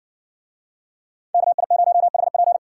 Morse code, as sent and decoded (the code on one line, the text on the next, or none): -.- .. ----- .... .--.
KI0HP